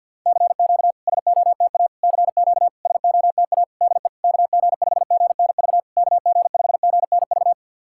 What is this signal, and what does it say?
Morse code, heard as a keyed tone, about 51 wpm, 700 Hz. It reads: CQ SOTA CQ SOTA DE KG5GNV KG5GNV
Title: SOTA QSO Practice 50wpm